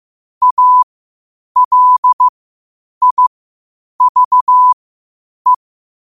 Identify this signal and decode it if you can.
Morse code, heard as a keyed tone, about 15 wpm, spaced as for 8 wpm, 1000 Hz. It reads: ALIVE
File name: qotdfiles / 8wpm